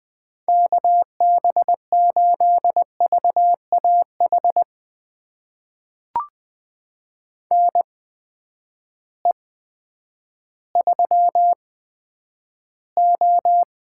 Morse code, heard as a keyed tone, about 20 wpm, spaced as for 10 wpm, 700 Hz. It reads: KB8VAH NE3O
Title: USA Call Signs - 8x Character Spacing - 20wpm